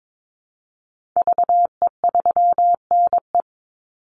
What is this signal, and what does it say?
Morse code, heard as a keyed tone, about 22 wpm, 700 Hz. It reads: VE3NE
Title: Call Signs Repeated 2x - 22wpm